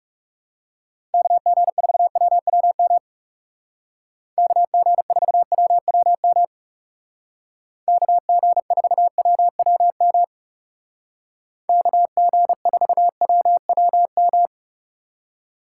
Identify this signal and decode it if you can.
Morse code, heard as a keyed tone, about 37 wpm, 700 Hz. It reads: KG4WWM KG4WWM KG4WWM KG4WWM
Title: Speed Racing - USA 2x3 Call Signs 30wpm